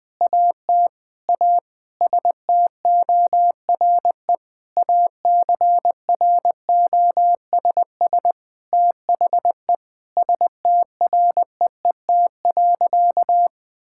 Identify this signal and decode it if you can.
Morse code, heard as a keyed tone, about 20 wpm, 700 Hz. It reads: AT A STORE ACROSS THE STREET.
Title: Sentences from Top 1000 Words - Review Part 5 - 20wpm